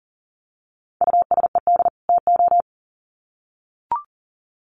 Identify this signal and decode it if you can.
Morse code, heard as a keyed tone, about 40 wpm, 700 Hz. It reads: USED TO
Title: Sets of 2 Words 40wpm